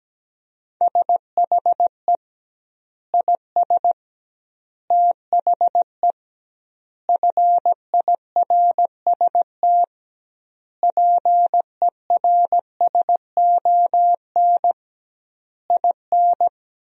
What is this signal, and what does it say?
Morse code, heard as a keyed tone, about 17 wpm, 700 Hz. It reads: SHE IS THE FIRST PERSON IN
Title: 2x Word-Spacing - Sentences from Top 400 Words - 17wpm